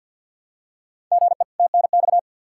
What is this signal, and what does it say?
Morse code, heard as a keyed tone, about 52 wpm, 700 Hz. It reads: GE TNX
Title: Practice QSO - Exchange 1 - Beginning - 50wpm